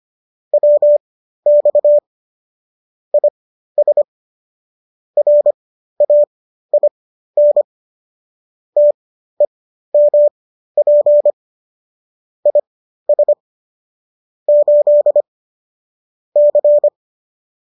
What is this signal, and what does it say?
Morse code, heard as a keyed tone, about 25 wpm, spaced as for 13 wpm, 590 Hz. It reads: WX IS RAIN TEMP IS 8 C